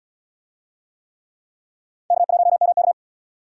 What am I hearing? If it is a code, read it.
Morse code, heard as a keyed tone, about 77 wpm, 700 Hz. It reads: K1MK